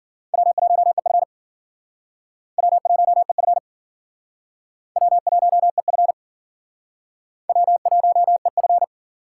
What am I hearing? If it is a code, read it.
Morse code, heard as a keyed tone, about 50 wpm, 700 Hz. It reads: W1EF W1EF W1EF W1EF